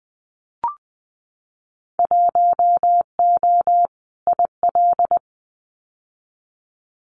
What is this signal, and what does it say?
Morse code, heard as a keyed tone, about 20 wpm, 700 Hz. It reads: 1O IL
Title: Winter Field Day Exchanges 20wpm